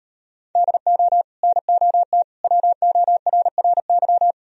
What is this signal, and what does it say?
Morse code, heard as a keyed tone, about 38 wpm, 700 Hz. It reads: DO NOT WORRY